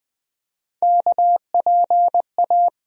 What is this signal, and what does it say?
Morse code, heard as a keyed tone, about 20 wpm, 700 Hz. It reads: KPA
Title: USA Call Sign Postfixes - 3 Letters 20wpm